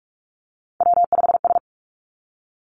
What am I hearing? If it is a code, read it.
Morse code, heard as a keyed tone, about 46 wpm, 700 Hz. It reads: W5S
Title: Three Challenging Letter-Number Combinations 45wpm